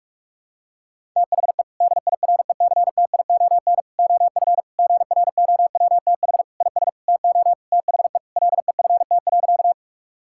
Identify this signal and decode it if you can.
Morse code, heard as a keyed tone, about 42 wpm, 700 Hz. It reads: THE DIRECTION OF GROWTH IS TO THE LEFT.
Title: Sentences from Top 1000 Words - Review Part 5 - 45wpm